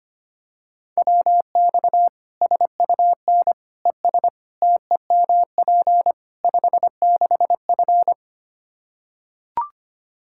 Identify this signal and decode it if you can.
Morse code, heard as a keyed tone, about 25 wpm, 700 Hz. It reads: WX SUN ES TEMP 56F
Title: Practice QSO - Exchange 2 - Info - 25wpm